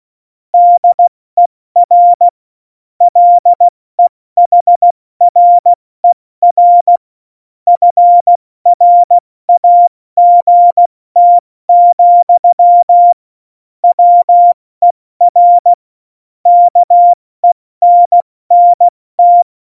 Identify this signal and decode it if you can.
Morse code, heard as a keyed tone, about 16 wpm, 700 Hz. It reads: DER LEHRER FRAGT, WER KENNT
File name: Morsekurs / Witze / Witz 19. November 2019, 70 F80